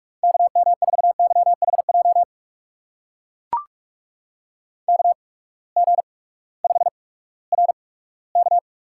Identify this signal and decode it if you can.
Morse code, heard as a keyed tone, about 44 wpm, spaced as for 20 wpm, 700 Hz. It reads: KM4YHJ KG5RK
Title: USA Call Signs - 8x Character Spacing - 45wpm